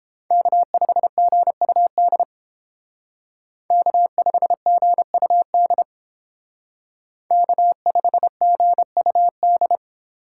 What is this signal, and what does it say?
Morse code, heard as a keyed tone, about 30 wpm, 700 Hz. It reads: K5GUD K5GUD K5GUD